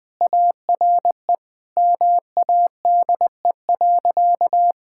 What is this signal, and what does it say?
Morse code, heard as a keyed tone, about 20 wpm, 700 Hz. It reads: ARE MADE.
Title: Sentences with 4-Word Phrases - 20wpm